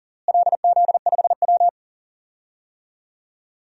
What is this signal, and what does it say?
Morse code, heard as a keyed tone, about 40 wpm, 700 Hz. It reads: LZ5W
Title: DXCC-100 Call Signs - 4 Characters - Repeated 2x - 40wpm